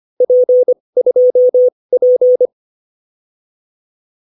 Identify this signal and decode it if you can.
Morse code, heard as a keyed tone, about 25 wpm, spaced as for 20 wpm, 560 Hz. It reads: P2P